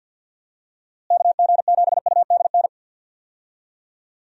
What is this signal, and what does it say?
Morse code, heard as a keyed tone, about 51 wpm, 700 Hz. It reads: KG7UDN
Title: USA Call Signs - 6 Characters - Repeated 3x - 50wpm